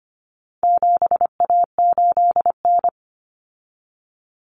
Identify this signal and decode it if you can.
Morse code, heard as a keyed tone, about 25 wpm, 700 Hz. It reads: 7A8N
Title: Common Cut Numbers - 4-Digit 25wpm